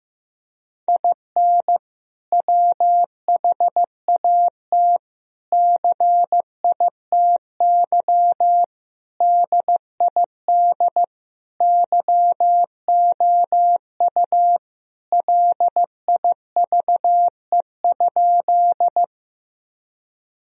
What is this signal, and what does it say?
Morse code, heard as a keyed tone, about 15 wpm, 700 Hz. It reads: IN WHAT CITY DID YOU LIVE?
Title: Sentences from Top 1000 Words - Review Part 1 - 15wpm